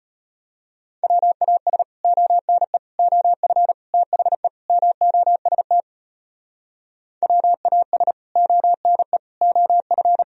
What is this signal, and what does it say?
Morse code, heard as a keyed tone, about 33 wpm, 700 Hz. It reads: WAS ONE OF THE MOST WAS ONE OF